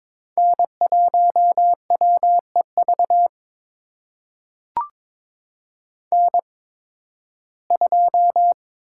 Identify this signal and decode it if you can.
Morse code, heard as a keyed tone, about 21 wpm, spaced as for 14 wpm, 700 Hz. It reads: N1WEV N2
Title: USA Call Signs - 5 Characters - 8x Character Spacing - 22wpm